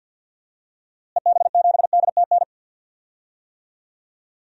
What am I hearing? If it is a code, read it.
Morse code, heard as a keyed tone, about 52 wpm, 700 Hz. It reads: EB7DTN
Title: DXCC-100 Call Signs - 6 Characters - Repeated 2x - 50wpm